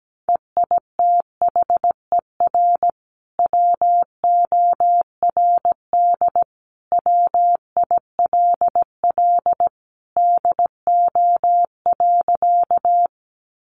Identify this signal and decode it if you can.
Morse code, heard as a keyed tone, about 17 wpm, 700 Hz. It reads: EITHER WORD WILL DO.
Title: Sentences from Top 600 Words 17wpm